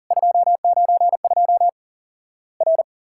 Morse code, heard as a keyed tone, about 39 wpm, 700 Hz. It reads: 292 R
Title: Worked All Europe DX Contest - QTC Exchange - 40wpm